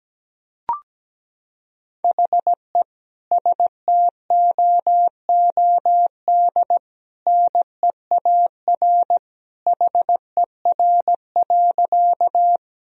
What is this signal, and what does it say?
Morse code, heard as a keyed tone, about 17 wpm, 700 Hz. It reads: HE STOOD NEAR HER.
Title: Sentences from Top 1000 Words - Review Part 2 - 17wpm